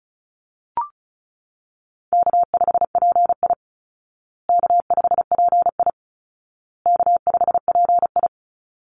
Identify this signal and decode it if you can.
Morse code, heard as a keyed tone, about 34 wpm, 700 Hz. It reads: K5PI K5PI K5PI
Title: Easy Call Signs Repeated 3x - 35wpm